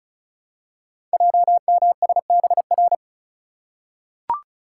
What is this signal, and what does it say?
Morse code, heard as a keyed tone, about 36 wpm, 700 Hz. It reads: JMSBR
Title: Code Groups - 5 Letters - 35wpm